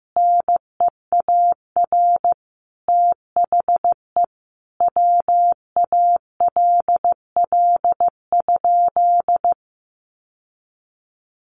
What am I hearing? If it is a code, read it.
Morse code, heard as a keyed tone, about 15 wpm, 700 Hz. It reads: NEAR THE WALL?
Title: Sentences from Top 1100 Words 15wpm